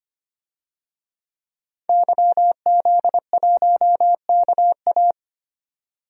Morse code, heard as a keyed tone, about 25 wpm, 700 Hz. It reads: YZ1KA
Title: DXCC-100 Call Signs - 5 Characters - 25wpm